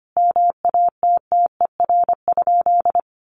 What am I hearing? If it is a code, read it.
Morse code, heard as a keyed tone, about 25 wpm, 700 Hz. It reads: MATTER?
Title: Sentences from Top 1300 Words 25wpm